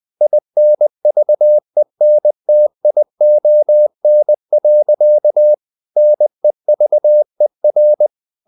Morse code, heard as a keyed tone, about 20 wpm, 600 Hz. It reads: INVENTION. NEVER
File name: CW / Practice / EnglishSayings20